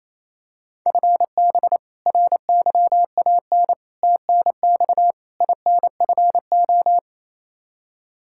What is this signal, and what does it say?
Morse code, heard as a keyed tone, about 28 wpm, 700 Hz. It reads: FB RYAN TNX INFO
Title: Practice QSO - Exchange 3 - More Info - 28wpm